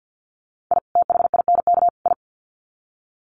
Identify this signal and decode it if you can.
Morse code, heard as a keyed tone, about 50 wpm, 700 Hz. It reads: I THINK I
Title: Sets of 3 Words 50wpm